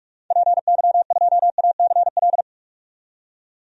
Morse code, heard as a keyed tone, about 42 wpm, 700 Hz. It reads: PY2ACL
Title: DXCC-100 Call Signs - 6 Characters - 45wpm